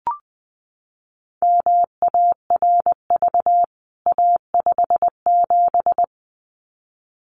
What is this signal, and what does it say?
Morse code, heard as a keyed tone, about 19 wpm, 700 Hz. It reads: MARV A57
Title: CWT Exchange 20wpm